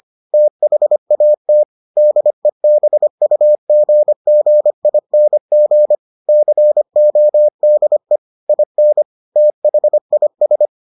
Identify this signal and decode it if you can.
Morse code, heard as a keyed tone, about 25 wpm, 600 Hz. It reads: THAT DEBUGGING CODE IN THIS